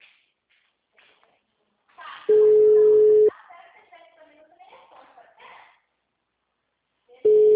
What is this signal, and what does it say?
Ringback tone (Europe, 425 Hz), with speech.